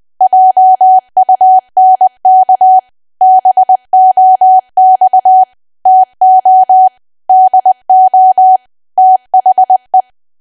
Morse code, heard as a keyed tone, about 20 wpm, 750 Hz. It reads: JUNK BOX TO DO THE
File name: cwa / W1AW-Level3 / 20WPM